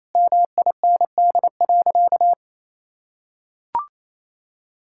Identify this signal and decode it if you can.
Morse code, heard as a keyed tone, about 28 wpm, 700 Hz. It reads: MIND.
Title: Sentences from Top 1000 Words - Review Part 6 - 28wpm